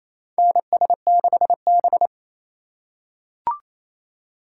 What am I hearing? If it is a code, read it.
Morse code, heard as a keyed tone, about 28 wpm, 700 Hz. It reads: NS6B
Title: USA Call Signs - 4 Characters - 2x Character Spacing - 28wpm